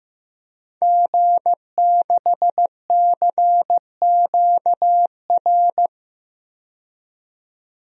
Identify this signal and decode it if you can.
Morse code, heard as a keyed tone, about 15 wpm, 700 Hz. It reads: G6CQR
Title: DXCC-100 Call Signs - 5 Characters - 15wpm